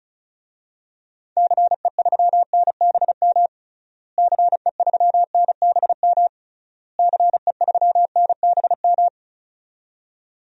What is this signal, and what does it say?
Morse code, heard as a keyed tone, about 36 wpm, 700 Hz. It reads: CE3NBM CE3NBM CE3NBM
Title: DXCC-100 Call Signs - 6 Characters - Repeated 3x - 35wpm